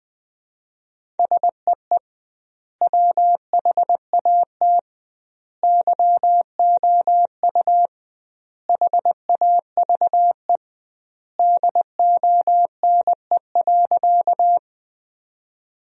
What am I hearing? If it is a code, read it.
Morse code, heard as a keyed tone, about 20 wpm, 700 Hz. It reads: SEE WHAT YOU HAVE DONE.
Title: 2x Word-Spacing - Sentences from Top 500 Words - 20wpm